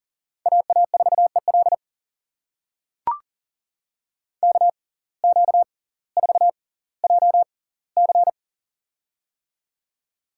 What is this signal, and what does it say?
Morse code, heard as a keyed tone, about 38 wpm, spaced as for 20 wpm, 700 Hz. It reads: AA4EL KQ4JC
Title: USA Call Signs - 5 Characters - 6x Character Spacing - 40wpm